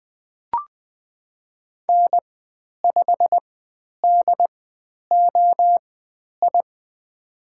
Morse code, heard as a keyed tone, about 20 wpm, spaced as for 10 wpm, 700 Hz. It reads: N5DOI
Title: 095 Lesson 16g - Letter c - Call Signs 20wpm